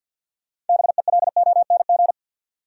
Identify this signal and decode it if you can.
Morse code, heard as a keyed tone, about 51 wpm, 700 Hz. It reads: BELONG
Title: Six Letter Words - 50wpm